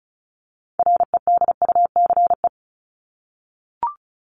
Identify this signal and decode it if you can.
Morse code, heard as a keyed tone, about 34 wpm, 700 Hz. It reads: REDUCE
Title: ED Suffix - Top 100 Words with Base Words 35wpm